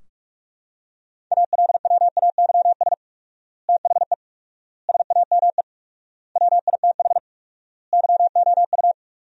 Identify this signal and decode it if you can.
Morse code, heard as a keyed tone, about 46 wpm, 700 Hz. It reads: ALWAYS THE SAME WITH YOU
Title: 4x Word-Spacing - Sentences from Top 300 Words - 45wpm